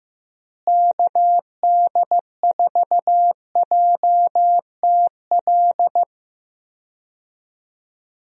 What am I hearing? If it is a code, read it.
Morse code, heard as a keyed tone, about 15 wpm, 700 Hz. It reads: KD4JTL
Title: USA Call Signs - 6 Characters - Repeated 2x - 15wpm